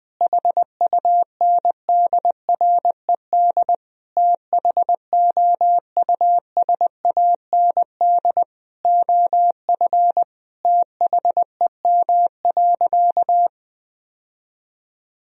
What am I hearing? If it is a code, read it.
Morse code, heard as a keyed tone, about 20 wpm, 700 Hz. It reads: HUNDRED THOUSAND OF THEM.
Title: Sentences from Top 1000 Words - Review Part 3 - 20wpm